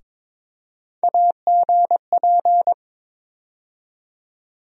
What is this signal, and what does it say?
Morse code, heard as a keyed tone, about 22 wpm, 700 Hz. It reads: AGP
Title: USA Call Sign Postfixes - 3 Letters 22wpm